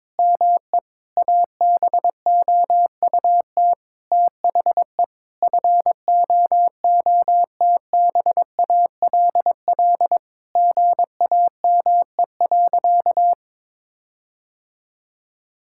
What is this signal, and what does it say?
Morse code, heard as a keyed tone, about 22 wpm, 700 Hz. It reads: ME ABOUT THE FOOTBALL GAME.
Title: Sentences from Top 1600 Words 22wpm